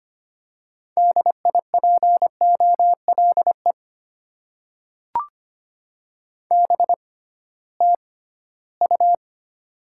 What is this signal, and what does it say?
Morse code, heard as a keyed tone, about 25 wpm, spaced as for 16 wpm, 700 Hz. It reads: DIPOLE BTU
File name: Top 90 QSO Elements - 6x Character Spacing - 25wpm